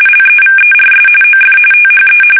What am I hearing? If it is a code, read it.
RTTY, about 45.45 baud. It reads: LEVERS OF THE